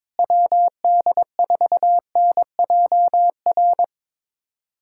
Morse code, heard as a keyed tone, about 22 wpm, 700 Hz. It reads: WD4NJR
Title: USA Callsign DE Callsign - 6 Characters - 22wpm